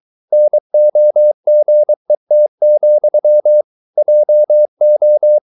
Morse code, heard as a keyed tone, about 23 wpm, 600 Hz. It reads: NOGET, JO
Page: Ferie 23 wpm